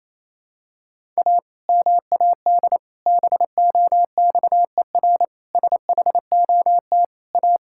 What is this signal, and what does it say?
Morse code, heard as a keyed tone, about 28 wpm, 700 Hz. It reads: A MAD BOXER SHOT A
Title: Pangram Challenge v2 - 28wpm